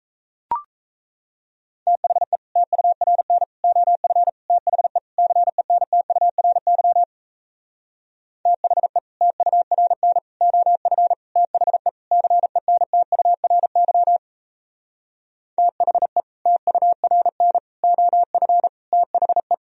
Morse code, heard as a keyed tone, about 37 wpm, 700 Hz. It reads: THE TURN OF THE CENTURY THE TURN OF THE CENTURY THE TURN OF THE